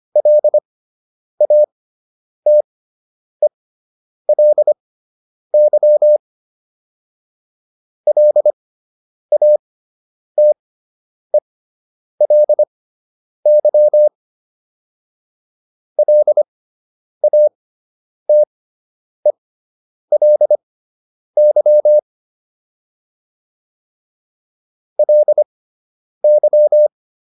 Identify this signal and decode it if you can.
Morse code, heard as a keyed tone, about 25 wpm, spaced as for 9 wpm, 600 Hz. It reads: LATELY LATELY LATELY LY